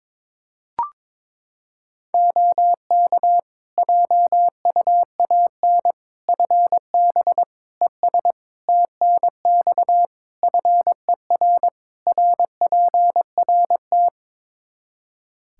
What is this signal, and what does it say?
Morse code, heard as a keyed tone, about 21 wpm, 700 Hz. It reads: OK JUAN FB ES TNX FER RPRT
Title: K1USN SST QSO Practice 22wpm